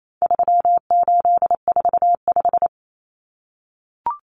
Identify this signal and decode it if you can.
Morse code, heard as a keyed tone, about 28 wpm, 700 Hz. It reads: 3845
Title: Common Cut Numbers - 4-Digit 28wpm